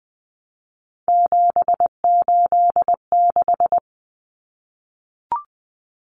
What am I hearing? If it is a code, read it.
Morse code, heard as a keyed tone, about 20 wpm, 700 Hz. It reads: 786
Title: Common Cut Numbers - 3-Digit 20wpm